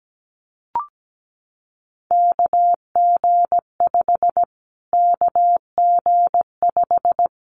Morse code, heard as a keyed tone, about 17 wpm, 700 Hz. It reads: KG5 KG5
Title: Callsign Prefixes for Top 25 DXCC - Repeated 3x - 17wpm